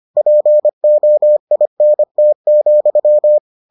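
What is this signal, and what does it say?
Morse code, heard as a keyed tone, about 25 wpm, 600 Hz. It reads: POINT,
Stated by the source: Morse edition of the Foundations of Amateur Radio episodes.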